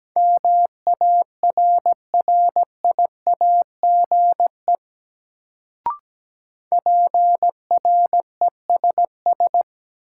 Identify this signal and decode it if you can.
Morse code, heard as a keyed tone, about 17 wpm, 700 Hz. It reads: MARRIAGE PRESS
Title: Sentences from Top 800 Words - New Word Drill - 17wpm